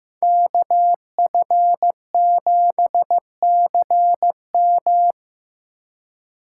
Morse code, heard as a keyed tone, about 15 wpm, 700 Hz. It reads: KF7CM
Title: USA Call Signs - 5 Characters - 4x Character Spacing - 15wpm